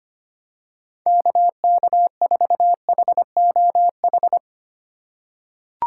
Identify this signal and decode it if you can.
Morse code, heard as a keyed tone, about 25 wpm, 700 Hz. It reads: KK4HOH
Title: USA Call Signs - 6 Characters - 2x Character Spacing - 25wpm